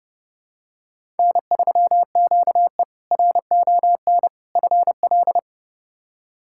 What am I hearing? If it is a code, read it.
Morse code, heard as a keyed tone, about 30 wpm, 700 Hz. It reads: N3QE RON FL
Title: K1USN SST QSO Practice - 30wpm